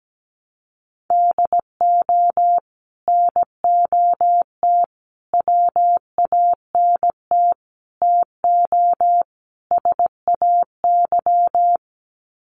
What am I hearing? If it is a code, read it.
Morse code, heard as a keyed tone, about 17 wpm, 700 Hz. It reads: DO NOT WANT TO SAY
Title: Sets of 5 Words Encore 17wpm